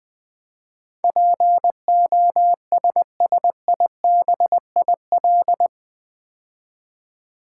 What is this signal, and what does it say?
Morse code, heard as a keyed tone, about 20 wpm, 700 Hz. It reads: POSSIBIL
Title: Build A Word - 4 Letters At A Time - 11-Letter Words - 20wpm